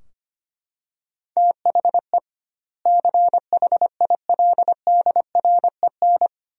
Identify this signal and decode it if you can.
Morse code, heard as a keyed tone, about 25 wpm, 700 Hz. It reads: THE CHILDREN
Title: 2x Word-Spacing - Sentences from Top 300 Words - 25wpm